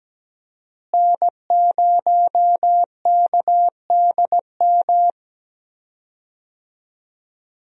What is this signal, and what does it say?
Morse code, heard as a keyed tone, about 17 wpm, 700 Hz. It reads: N0KDM
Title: USA Call Signs - 5 Characters - Repeated 2x - 17wpm